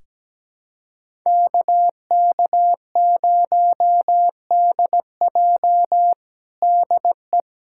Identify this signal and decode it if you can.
Morse code, heard as a keyed tone, about 17 wpm, 700 Hz. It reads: KK0DJ DE
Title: USA Callsign DE Callsign - 5 Characters - 17wpm